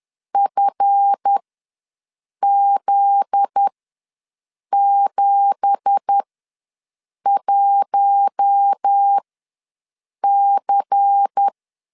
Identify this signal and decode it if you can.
Morse code, heard as a keyed tone, about 10 wpm, spaced as for 6 wpm, 800 Hz. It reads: FZ71C
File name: LEZIONI / 9 2867 ws 10wpm 4215